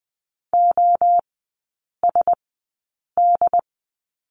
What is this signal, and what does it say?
Morse code, heard as a keyed tone, about 20 wpm, spaced as for 8 wpm, 700 Hz. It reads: OSD
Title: 087 Lesson 15e - Number 5 - Three Characters 20wpm